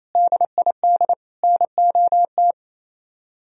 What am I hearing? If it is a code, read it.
Morse code, heard as a keyed tone, about 34 wpm, 700 Hz. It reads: DID NOT
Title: Speed-Racing - Sets of 3 Words 28wpm